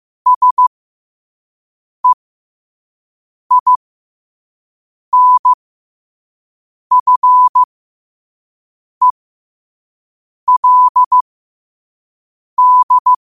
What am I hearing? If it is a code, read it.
Morse code, heard as a keyed tone, about 15 wpm, spaced as for 5 wpm, 1000 Hz. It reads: SEINFELD
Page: New quotes every day in morse code at 5 Words per minute.